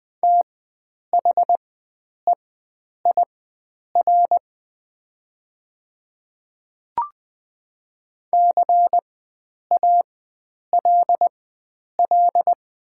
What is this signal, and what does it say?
Morse code, heard as a keyed tone, about 20 wpm, spaced as for 9 wpm, 700 Hz. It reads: THEIR CALL
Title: 4x Character Spacing - Top 300 Words - 20wpm